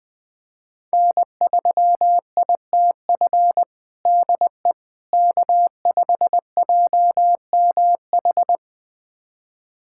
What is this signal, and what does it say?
Morse code, heard as a keyed tone, about 21 wpm, 700 Hz. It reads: N3ITF DE K5JMH
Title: USA Callsign DE Callsign - 5 Characters - 20wpm